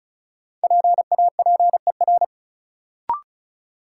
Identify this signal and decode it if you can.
Morse code, heard as a keyed tone, about 35 wpm, 700 Hz. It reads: PAPER
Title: Sentences from Top 300 Words - New Word Drill - 35wpm